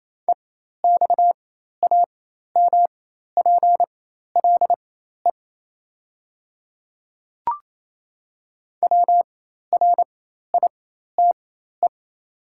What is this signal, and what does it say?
Morse code, heard as a keyed tone, about 28 wpm, spaced as for 13 wpm, 700 Hz. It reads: EXAMPLE WRITE
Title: 4x Character Spacing - Top 300 Words - 28wpm